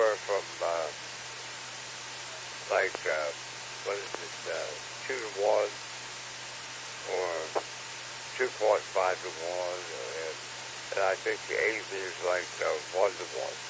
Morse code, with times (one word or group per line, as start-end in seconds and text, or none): none